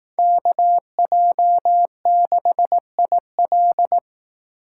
Morse code, heard as a keyed tone, about 18 wpm, 700 Hz. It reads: KJ6IL
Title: Speed Racing - USA 2x2 Call Signs 15wpm